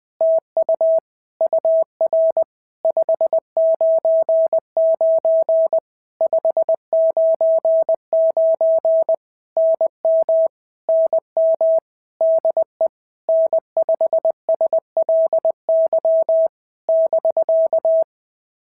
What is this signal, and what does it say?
Morse code, heard as a keyed tone, about 20 wpm, 650 Hz. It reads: TU UR 599 599 NM NM DE N5SLY <BK>
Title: POTA QSO Practice - 20wpm